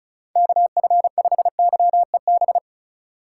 Morse code, heard as a keyed tone, about 36 wpm, 700 Hz. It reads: KF5YEB
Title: USA Call Signs - 6 Characters - 2x Character Spacing - 35wpm